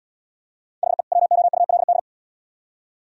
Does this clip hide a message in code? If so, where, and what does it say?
Yes: Morse code from 0.8 to 2.0 s, heard as a keyed tone, about 78 wpm, 700 Hz. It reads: HE WOULD